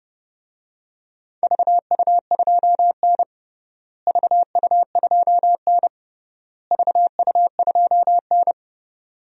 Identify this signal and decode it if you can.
Morse code, heard as a keyed tone, about 30 wpm, 700 Hz. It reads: VU2N VU2N VU2N
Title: DXCC-100 Call Signs - 4 Characters - Repeated 3x - 30wpm